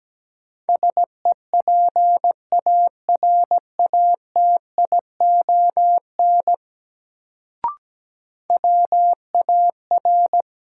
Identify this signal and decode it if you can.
Morse code, heard as a keyed tone, about 17 wpm, 700 Hz. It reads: SEPARATION WAR
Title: Sets of 5 Words - New Word Drill - 17wpm